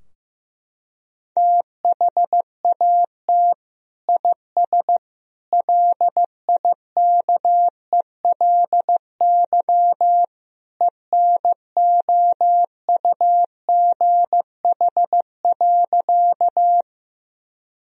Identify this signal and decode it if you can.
Morse code, heard as a keyed tone, about 15 wpm, 700 Hz. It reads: THAT IS LIKELY ENOUGH.
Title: Sentences from Top 1000 Words - Review Part 4 - 15wpm